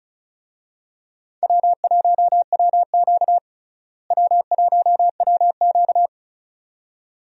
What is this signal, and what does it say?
Morse code, heard as a keyed tone, about 36 wpm, 700 Hz. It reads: W1WQ W1WQ
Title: USA Call Signs - 4 Characters - Repeated 2x - 35wpm